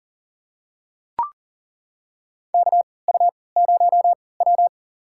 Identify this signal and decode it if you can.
Morse code, heard as a keyed tone, about 40 wpm, spaced as for 23 wpm, 700 Hz. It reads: KU0W
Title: USA Call Signs - 4 Characters - 3x Character Spacing - 40wpm